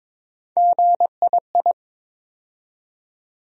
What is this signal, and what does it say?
Morse code, heard as a keyed tone, about 22 wpm, 700 Hz. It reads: GII
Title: USA Call Sign Postfixes - 3 Letters 22wpm